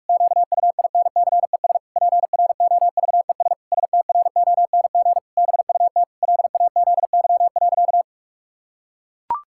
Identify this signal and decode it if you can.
Morse code, heard as a keyed tone, about 41 wpm, 700 Hz. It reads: QUINCES PROVES STRONG BUT LAZY.
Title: Pangram Challenge v2 - 45wpm